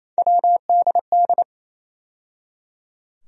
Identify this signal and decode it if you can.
Morse code, heard as a keyed tone, about 28 wpm, 700 Hz. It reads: WDD
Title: USA Call Sign Postfixes - 3 Letters 28wpm